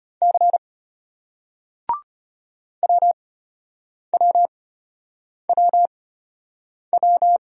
Morse code, heard as a keyed tone, about 34 wpm, spaced as for 8 wpm, 700 Hz. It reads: C WWWW